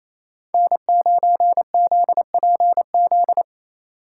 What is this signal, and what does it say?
Morse code, heard as a keyed tone, about 37 wpm, 700 Hz. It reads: N9ZPZ